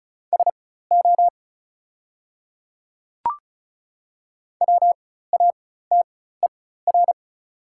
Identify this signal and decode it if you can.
Morse code, heard as a keyed tone, about 36 wpm, spaced as for 16 wpm, 690 Hz. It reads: SO WATER
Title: 4x Character Spacing - Top 100 Words - 35wpm